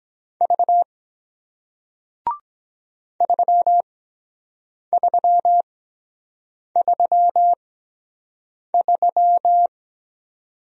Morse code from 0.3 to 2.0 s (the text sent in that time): V